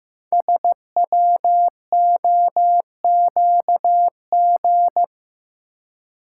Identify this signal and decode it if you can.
Morse code, heard as a keyed tone, about 15 wpm, 700 Hz. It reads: SWOQG
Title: Code Groups - 5 Letters - 15wpm